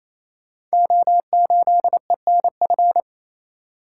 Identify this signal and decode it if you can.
Morse code, heard as a keyed tone, about 29 wpm, 700 Hz. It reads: O8ENF
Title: Code Groups - 5 Letter:Numbers - 28wpm